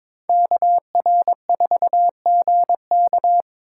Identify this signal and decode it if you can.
Morse code, heard as a keyed tone, about 22 wpm, 700 Hz. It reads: KR4GK
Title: USA Call Signs - 5 Characters - 4x Character Spacing - 22wpm